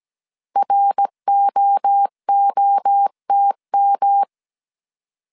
Morse code, heard as a keyed tone, about 17 wpm, 800 Hz. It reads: ROOTM